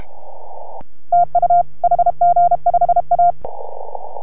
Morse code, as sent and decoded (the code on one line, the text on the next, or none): - ..- / .... --. ..... .-
TU HG5A